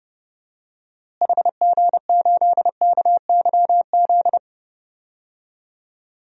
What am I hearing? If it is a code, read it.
Morse code, heard as a keyed tone, about 31 wpm, 700 Hz. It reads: HG8KYZ
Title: DXCC-100 Call Signs - 6 Characters - Repeated 2x - 30wpm